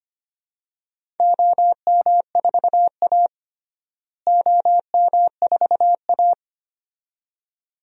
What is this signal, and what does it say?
Morse code, heard as a keyed tone, about 25 wpm, 700 Hz. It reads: OM4A OM4A
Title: DXCC-100 Call Signs - 4 Characters - Repeated 2x - 25wpm